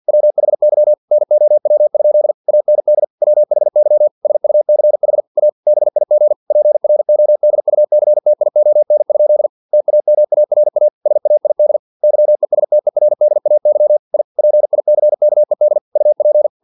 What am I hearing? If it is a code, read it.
Morse code, heard as a keyed tone, about 50 wpm, 600 Hz. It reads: WHY NOW? AND WHY SUCH A BIG PRODUCTION? TAMARA SAID YESTERDAY I PICKED UP
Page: Never 49wpm